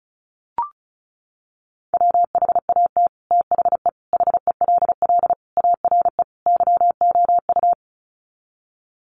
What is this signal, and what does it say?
Morse code, heard as a keyed tone, about 34 wpm, 700 Hz. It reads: WHAT THE HELL ARE YOU
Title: Sets of 5 Words Encore 35wpm